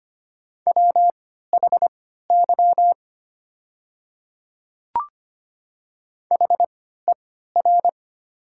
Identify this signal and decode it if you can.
Morse code, heard as a keyed tone, about 25 wpm, spaced as for 14 wpm, 700 Hz. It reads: WHY HER
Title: 3x Character Spacing - Top 200 Words - 25wpm